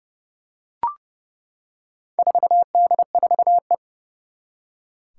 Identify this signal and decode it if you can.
Morse code, heard as a keyed tone, about 30 wpm, 700 Hz. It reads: 4D4E
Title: Four Challenging Letter-Number Combinations 30wpm